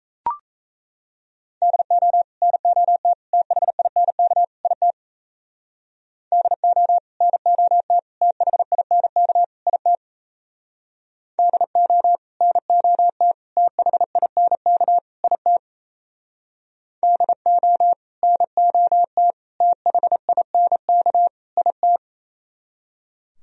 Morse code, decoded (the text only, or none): DO NOT THINK IT DO NOT THINK IT DO NOT THINK IT DO NOT THINK IT